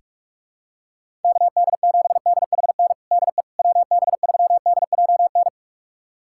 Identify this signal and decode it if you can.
Morse code, heard as a keyed tone, about 42 wpm, 700 Hz. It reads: KD7DHN DE WB3DJN
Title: USA Callsign DE Callsign 45wpm